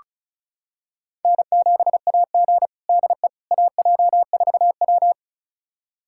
Morse code, heard as a keyed tone, about 36 wpm, 700 Hz. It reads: N7AG DE AJ4W
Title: USA Callsign DE Callsign - 4 Characters - 35wpm